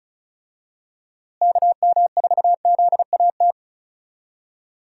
Morse code, heard as a keyed tone, about 36 wpm, 700 Hz. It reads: KM4ZAT
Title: USA Call Signs - 6 Characters - Repeated 3x - 35wpm